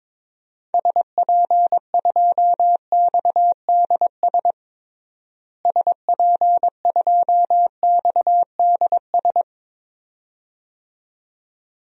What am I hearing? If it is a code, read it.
Morse code, heard as a keyed tone, about 22 wpm, 700 Hz. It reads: SP2XDS SP2XDS
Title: DXCC-100 Call Signs - 6 Characters - Repeated 2x - 22wpm